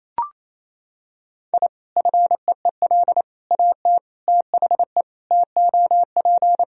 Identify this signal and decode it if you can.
Morse code, heard as a keyed tone, about 28 wpm, 700 Hz. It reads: I FEEL AT THE TOP
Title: Sentences from Top 1000 Words - Review Part 2 - 28wpm